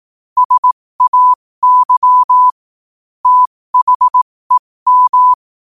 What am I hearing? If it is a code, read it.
Morse code, heard as a keyed tone, about 18 wpm, spaced as for 15 wpm, 1000 Hz. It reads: SAY THEM
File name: qotdfiles / 15wpm